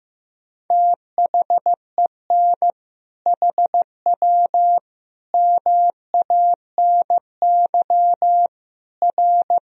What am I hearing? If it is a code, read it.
Morse code, heard as a keyed tone, about 15 wpm, 700 Hz. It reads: THEN HW MANY R
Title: Sentences from Top 100 Words and Abbreviations 15wpm